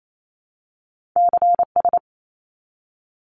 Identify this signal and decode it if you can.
Morse code, heard as a keyed tone, about 28 wpm, 700 Hz. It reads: CS
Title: Abbreviations 28wpm